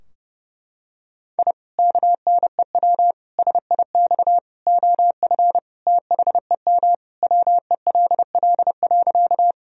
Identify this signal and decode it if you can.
Morse code, heard as a keyed tone, about 32 wpm, 700 Hz. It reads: I KNEW SIX OF THEM WELL.
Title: Sentences from Top 1000 Words - Review Part 2 - 30wpm